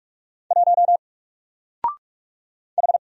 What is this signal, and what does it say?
Morse code, heard as a keyed tone, about 44 wpm, 700 Hz. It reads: 1 H
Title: Single Character (Letter:Number:Punctuation:Prosign) 45wpm